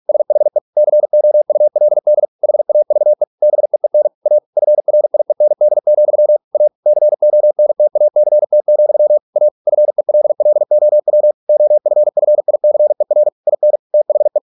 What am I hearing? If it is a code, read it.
Morse code, heard as a keyed tone, about 46 wpm, 600 Hz. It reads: SHE COULD HAVE BEEN A FRIEND, A CONTACT, A FELLOW OFFICER IN THE